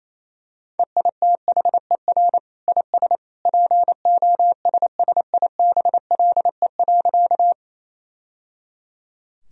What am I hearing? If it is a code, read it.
Morse code, heard as a keyed tone, about 28 wpm, 700 Hz. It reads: EITHER IS POSSIBLE.
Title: Sentences from Top 1000 Words - Review Part 4 - 28wpm